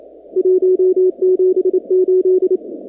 Morse code, heard as a keyed tone, about 28 wpm, 370 Hz. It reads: 178
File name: contests / 2019-cw-wpx-kt5j-k5pi- -2330 0-45